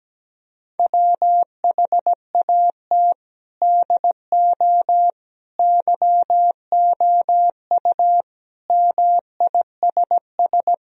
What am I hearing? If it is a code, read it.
Morse code, heard as a keyed tone, about 17 wpm, 700 Hz. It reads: WHAT DO YOU MISS
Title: Sentences from Top 1000 Words - Review Part 3 - 17wpm